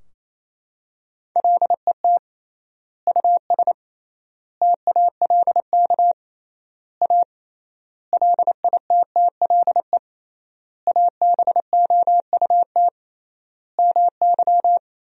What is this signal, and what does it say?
Morse code, heard as a keyed tone, about 28 wpm, 700 Hz. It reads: LET US TALK A LITTLE ABOUT MY
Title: 3x Word-Spacing - Sentences from Top 400 Words - 28wpm